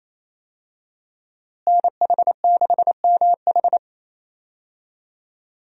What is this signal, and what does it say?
Morse code, heard as a keyed tone, about 28 wpm, 700 Hz. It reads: NH6MH
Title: USA Call Signs - 5 Characters - Repeated 2x - 28wpm